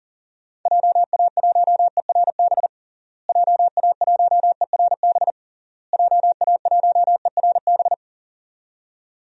Morse code, heard as a keyed tone, about 40 wpm, 700 Hz. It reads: JA1ERB JA1ERB JA1ERB
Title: DXCC-100 Call Signs - 6 Characters - Repeated 3x - 40wpm